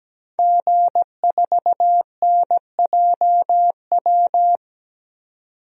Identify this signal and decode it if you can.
Morse code, heard as a keyed tone, about 17 wpm, 700 Hz. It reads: G4NJW
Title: DXCC-100 Call Signs - 5 Characters - 17wpm